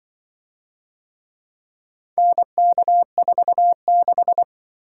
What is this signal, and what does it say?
Morse code, heard as a keyed tone, about 25 wpm, 700 Hz. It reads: NK46
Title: Speed-Racing - Four Letter-Number Combinations - 20wpm